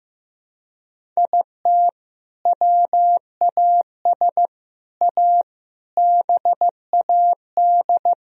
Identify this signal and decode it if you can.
Morse code, heard as a keyed tone, about 15 wpm, 700 Hz. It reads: IT WAS A BAD
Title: Sentences from Top 1000 Words - Review Part 1 - 15wpm